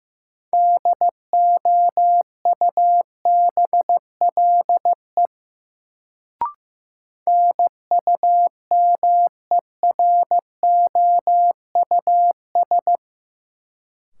Morse code, heard as a keyed tone, about 15 wpm, 700 Hz. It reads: DOUBLE NUMEROUS
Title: Sentences from Top 1400 Words - New Word Drill Alt - 15wpm